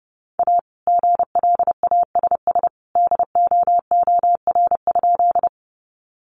Morse code, heard as a keyed tone, about 30 wpm, 700 Hz. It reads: A GLASS DOOR?
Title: Sentences from Top 1300 Words 30wpm